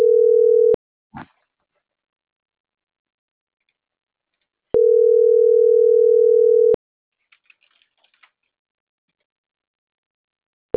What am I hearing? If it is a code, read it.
Ringback tone (North America), 440 + 480 Hz.
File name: CENTRAL NAVIERAS / out juan.salcedo